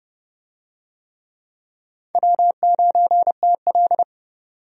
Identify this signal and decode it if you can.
Morse code, heard as a keyed tone, about 30 wpm, 700 Hz. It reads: W9TL
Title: USA Call Signs - 4 Characters - 30wpm